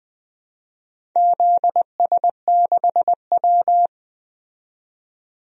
Morse code, heard as a keyed tone, about 20 wpm, 700 Hz. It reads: ZS6W
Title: DXCC-100 Call Signs - 4 Characters - 20wpm